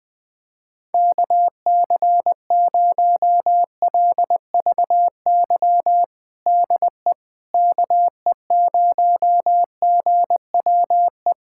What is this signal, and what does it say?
Morse code, heard as a keyed tone, about 20 wpm, 700 Hz. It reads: KC0LVY DE KE0GWE
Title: USA Callsign DE Callsign 20wpm